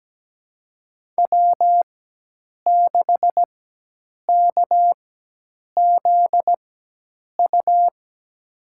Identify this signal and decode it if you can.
Morse code, heard as a keyed tone, about 17 wpm, spaced as for 8 wpm, 700 Hz. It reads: W6KZU
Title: USA Call Signs - 4x Character Spacing - 17wpm